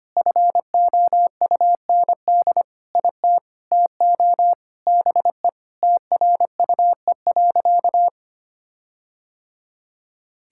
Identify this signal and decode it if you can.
Morse code, heard as a keyed tone, about 25 wpm, 700 Hz. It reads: FOUND IT TO BE TRUE.
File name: practice / 25wpm / Sentences with 4-Word Phrases - 25wpm